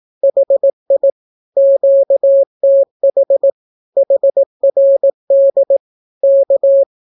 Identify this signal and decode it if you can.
Morse code, heard as a keyed tone, about 18 wpm, 550 Hz. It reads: HI QTH HRD K
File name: naukapodstaw / skroty / skroty